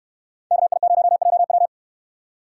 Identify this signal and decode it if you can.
Morse code, heard as a keyed tone, about 69 wpm, 700 Hz. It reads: KI0JW